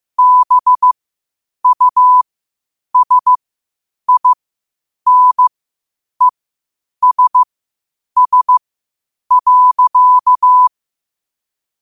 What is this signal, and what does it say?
Morse code, heard as a keyed tone, about 15 wpm, spaced as for 8 wpm, 1000 Hz. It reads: BUSINESS.
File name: qotdfiles / 8wpm